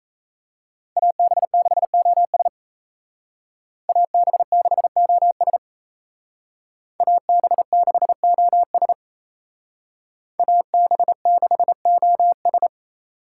Morse code, heard as a keyed tone, about 37 wpm, 700 Hz. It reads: AB6OS AB6OS AB6OS AB6OS